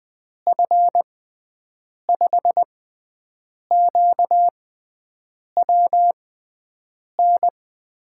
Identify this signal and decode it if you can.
Morse code, heard as a keyed tone, about 20 wpm, spaced as for 7 wpm, 700 Hz. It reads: F5QWN
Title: USA Call Signs - 6 Characters - 6x Character Spacing - 20wpm